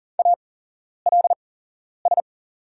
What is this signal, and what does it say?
Morse code, heard as a keyed tone, about 40 wpm, spaced as for 11 wpm, 700 Hz. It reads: ALS
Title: 8x Character Spacing - Top 200 Words - 40wpm